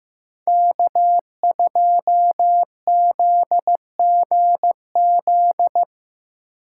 Morse code, heard as a keyed tone, about 17 wpm, 700 Hz. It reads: K2ZGZ
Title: Speed Racing - USA Call Signs 15wpm